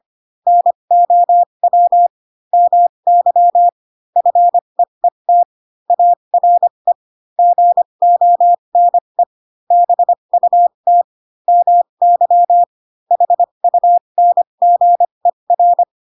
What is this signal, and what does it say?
Morse code, heard as a keyed tone, about 25 wpm, spaced as for 22 wpm, 700 Hz. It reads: NOW MY FEET ARE GONE BUT MY HUNGER
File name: Pinocchio / Pinocchio ch7 18 25 ik1hge